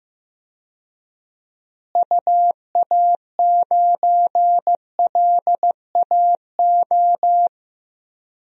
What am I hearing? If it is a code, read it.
Morse code, heard as a keyed tone, about 15 wpm, 700 Hz. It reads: UA9LAO
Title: DXCC-100 Call Signs - 6 Characters - Repeated 2x - 15wpm